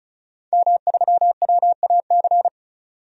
Morse code, heard as a keyed tone, about 34 wpm, 700 Hz. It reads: M3WAC
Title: DXCC-100 Call Signs - 5 Characters - 35wpm